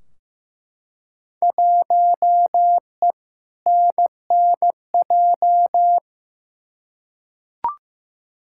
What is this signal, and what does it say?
Morse code, heard as a keyed tone, about 15 wpm, 700 Hz. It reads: 1E NNJ
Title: Field Day Exchanges - 15wpm